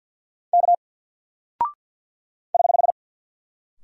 Morse code, heard as a keyed tone, about 50 wpm, 700 Hz. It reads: K <HH>
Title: Single Character (Letter:Number:Punctuation:Prosign) 50wpm